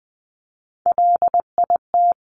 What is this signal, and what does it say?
Morse code, heard as a keyed tone, about 20 wpm, 700 Hz. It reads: LIT
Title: Top Three Letter Combinations in Words 20wpm